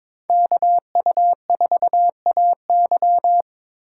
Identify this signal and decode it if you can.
Morse code, heard as a keyed tone, about 26 wpm, 700 Hz. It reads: KU4AY
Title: Speed Racing - USA 2x2 Call Signs 22wpm